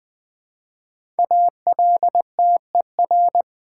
Morse code, heard as a keyed tone, about 20 wpm, 700 Hz. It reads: ALTER
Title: Build A Word - 1 Letter At A Time - 5-Letter Words - 20wpm